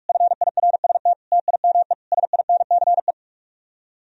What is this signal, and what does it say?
Morse code, heard as a keyed tone, about 50 wpm, 700 Hz. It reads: FIRST TIME SINCE